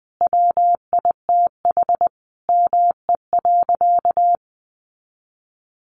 Morse code, heard as a keyed tone, about 20 wpm, 700 Hz. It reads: WITH ME.
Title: Sentences from Top 500 Words 20wpm